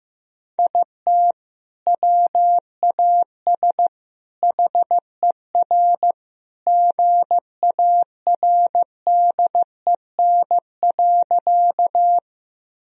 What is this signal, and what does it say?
Morse code, heard as a keyed tone, about 15 wpm, 700 Hz. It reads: IT WAS HER GARDEN.
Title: Sentences from Top 1000 Words - Review Part 4 - 15wpm